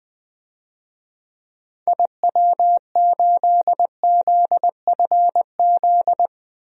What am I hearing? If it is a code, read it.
Morse code, heard as a keyed tone, about 20 wpm, 700 Hz. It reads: IW8ZFZ
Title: DXCC-100 Call Signs - 6 Characters - Repeated 3x - 20wpm